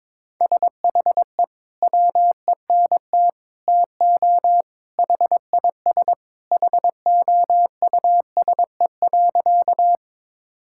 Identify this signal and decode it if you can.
Morse code, heard as a keyed tone, about 22 wpm, 700 Hz. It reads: SHE WENT TO HIS HOUSE.
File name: practice / 22wpm / Sentences with 2-Word Phrases - 22wpm